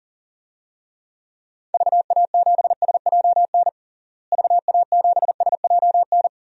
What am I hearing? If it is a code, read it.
Morse code, heard as a keyed tone, about 37 wpm, 700 Hz. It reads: VA7SJN VA7SJN
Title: DXCC-100 Call Signs - 6 Characters - Repeated 3x - 40wpm